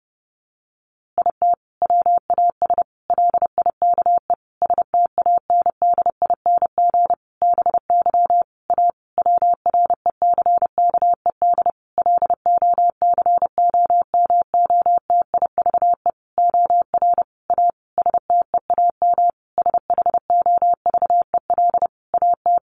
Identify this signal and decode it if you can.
Morse code, heard as a keyed tone, about 30 wpm, 700 Hz. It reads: IT WAS LIKE STANDING BY A WRECKED LOCOMOTIVE OR A STEAM SHOVEL AT
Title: A Sound Of Thunder 30wpm